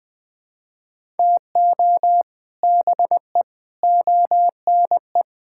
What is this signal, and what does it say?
Morse code, heard as a keyed tone, about 19 wpm, 700 Hz. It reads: TO BE ONE
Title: Sentences with 6-Word Phrases - 20wpm